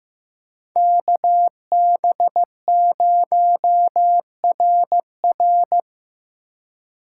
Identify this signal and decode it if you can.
Morse code, heard as a keyed tone, about 15 wpm, 700 Hz. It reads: KB0RR
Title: USA Call Signs - 2x Character Spacing - 15wpm